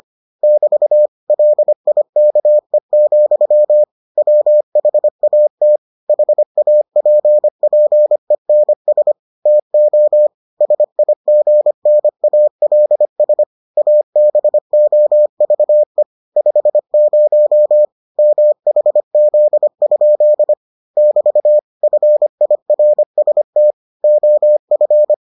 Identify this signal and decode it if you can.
Morse code, heard as a keyed tone, about 25 wpm, 600 Hz. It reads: <BT> LIKE, WHAT HAPPENS TO SIGNALS ABOVE 50 MHZ? <BT> FIRST OF